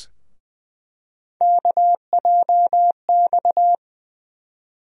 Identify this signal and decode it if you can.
Morse code, heard as a keyed tone, about 20 wpm, 700 Hz. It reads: KJX
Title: USA Call Sign Postfixes - 3 Letters 20wpm